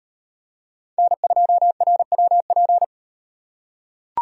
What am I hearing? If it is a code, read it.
Morse code, heard as a keyed tone, about 36 wpm, 700 Hz. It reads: N2RWP